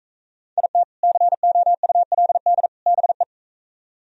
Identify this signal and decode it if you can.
Morse code, heard as a keyed tone, about 44 wpm, 700 Hz. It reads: IT COULD BE